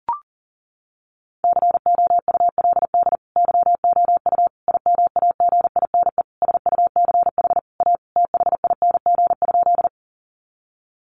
Morse code, heard as a keyed tone, about 40 wpm, 700 Hz. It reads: COULD YOU IMAGINE SUCH A THING?
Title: Sentences from Top 1400 Words 40wpm